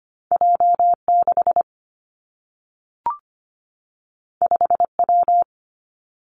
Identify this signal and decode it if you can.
Morse code, heard as a keyed tone, about 25 wpm, 700 Hz. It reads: J 6 5 W
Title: Two Challenging Letter-Number Combinations 25wpm